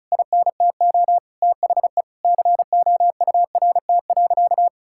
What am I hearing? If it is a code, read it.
Morse code, heard as a keyed tone, about 34 wpm, 700 Hz. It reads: INTO THE COURT.
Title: Sentences from Top 1600 Words 35wpm